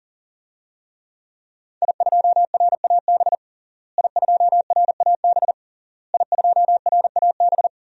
Morse code, heard as a keyed tone, about 40 wpm, 700 Hz. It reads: I2RAB I2RAB I2RAB
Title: DXCC-100 Call Signs - 5 Characters - Repeated 3x - 40wpm